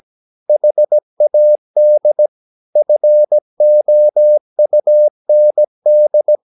Morse code, heard as a keyed tone, about 17 wpm, 600 Hz. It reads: HAD FOUND
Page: Romanen Never af Ken Follet på engelsk.